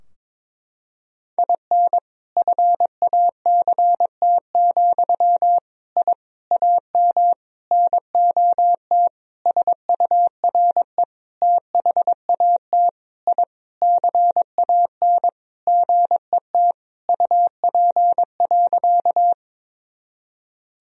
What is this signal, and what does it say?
Morse code, heard as a keyed tone, about 22 wpm, 700 Hz. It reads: IN FACT, I AM NOT SURE THAT I CAN GET UP.
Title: Sentences from Top 1000 Words - Review Part 3 - 22wpm